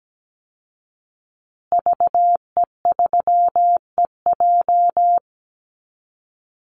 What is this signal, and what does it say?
Morse code, heard as a keyed tone, about 17 wpm, 700 Hz. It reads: VE3EJ
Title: Call Signs Repeated 2x - 17wpm